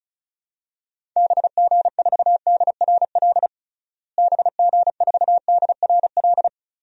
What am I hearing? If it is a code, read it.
Morse code, heard as a keyed tone, about 36 wpm, 700 Hz. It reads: BG4DRL BG4DRL
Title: DXCC-100 Call Signs - 6 Characters - Repeated 3x - 35wpm